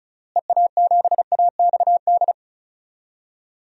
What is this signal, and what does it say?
Morse code, heard as a keyed tone, about 36 wpm, 700 Hz. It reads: EA7AXD
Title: DXCC-100 Call Signs - 6 Characters - Repeated 3x - 35wpm